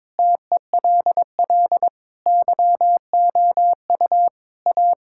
Morse code, heard as a keyed tone, about 23 wpm, 700 Hz. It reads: TELL YOU A
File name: practice / 22wpm / Sentences with 3-Word Phrases - 22wpm